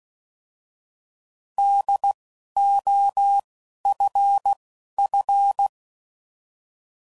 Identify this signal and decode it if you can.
Morse code, heard as a keyed tone, about 16 wpm, spaced as for 11 wpm, 800 Hz. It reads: DOFF
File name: Prefixes / By First Letter / Ds